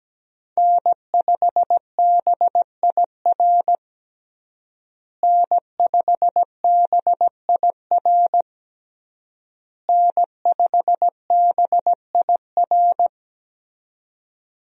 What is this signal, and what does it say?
Morse code, heard as a keyed tone, about 17 wpm, 700 Hz. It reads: N5BIR N5BIR N5BIR
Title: USA Call Signs - 5 Characters - Repeated 3x - 17wpm